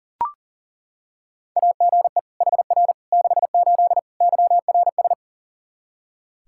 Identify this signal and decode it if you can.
Morse code, heard as a keyed tone, about 40 wpm, 700 Hz. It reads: AGE HR 68 YRS
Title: K1USN SST QSO Practice 40wpm